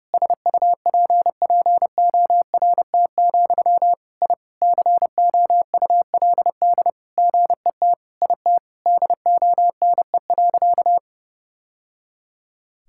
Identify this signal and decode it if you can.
Morse code, heard as a keyed tone, about 32 wpm, 700 Hz. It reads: SUPPORT, I COULD GET IT DONE.
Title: Sentences from Top 1000 Words - Review Part 5 - 30wpm